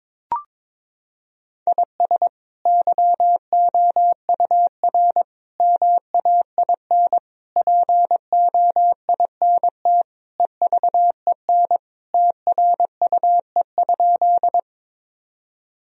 Morse code, heard as a keyed tone, about 22 wpm, 700 Hz. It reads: IS YOUR MAIN POINT EVEN TRUE?
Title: Sentences from Top 1000 Words - Review Part 2 - 22wpm